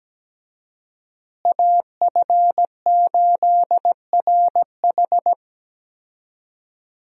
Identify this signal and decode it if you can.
Morse code, heard as a keyed tone, about 17 wpm, 700 Hz. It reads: AF8RH
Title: USA Call Signs - 5 Characters - Repeated 2x - 17wpm